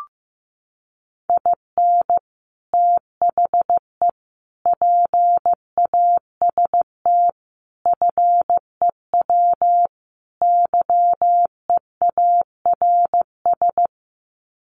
Morse code, heard as a keyed tone, about 14 wpm, 700 Hz. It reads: IN THE PAST FEW YEARS
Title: Sets of 5 Words 15wpm